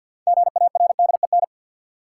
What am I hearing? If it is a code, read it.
Morse code, heard as a keyed tone, about 48 wpm, 700 Hz. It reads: GARDEN
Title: Sentences from Top 700 Words - New Word Drill - 50wpm